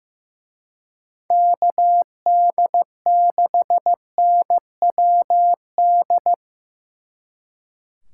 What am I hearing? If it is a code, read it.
Morse code, heard as a keyed tone, about 16 wpm, 700 Hz. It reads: KD6NWD
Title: Speed Racing - USA 2x3 Call Signs 15wpm